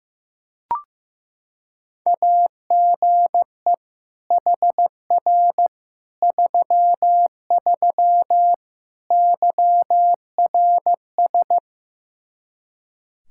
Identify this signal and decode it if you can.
Morse code, heard as a keyed tone, about 15 wpm, 700 Hz. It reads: AGE HR 33 YRS
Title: Practice QSO - Exchange 3 - More Info - 15wpm